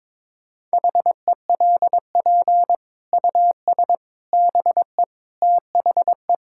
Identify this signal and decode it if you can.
Morse code, heard as a keyed tone, about 22 wpm, 700 Hz. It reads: HELP US BE THE
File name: practice / 22wpm / Sets of 8 Words - 22wpm